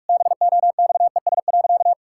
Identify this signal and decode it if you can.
Morse code, heard as a keyed tone, about 42 wpm, 700 Hz. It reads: BOXES.
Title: Pangram Challenge v2 - 45wpm